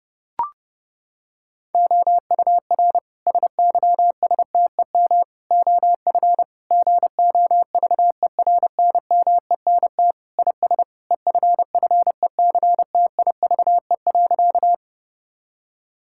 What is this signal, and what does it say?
Morse code, heard as a keyed tone, about 32 wpm, 700 Hz. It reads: OUR SYSTEM OF GOVERNMENT IS EFFECTIVE.
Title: Sentences from Top 1000 Words - Review Part 6 - 30wpm